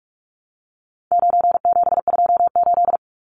Morse code, heard as a keyed tone, about 44 wpm, 700 Hz. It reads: 9728
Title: Numbers - 4-Digit 45wpm